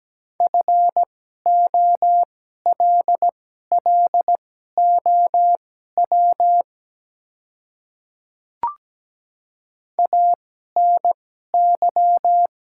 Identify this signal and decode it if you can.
Morse code, heard as a keyed tone, about 17 wpm, spaced as for 12 wpm, 700 Hz. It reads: FOLLOW ANY
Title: 2x Character Spacing - Top 300 Words - 17wpm